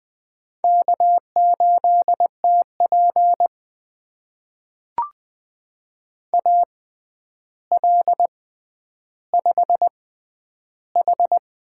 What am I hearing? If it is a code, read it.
Morse code, heard as a keyed tone, about 21 wpm, spaced as for 10 wpm, 700 Hz. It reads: K8TP AL5H
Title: USA Call Signs - 6x Character Spacing - 20wpm